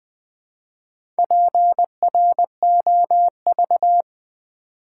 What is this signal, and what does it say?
Morse code, heard as a keyed tone, about 20 wpm, 700 Hz. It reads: PROV
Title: Build A Word - 2 Letters At A Time - 8-Letter Words - 20wpm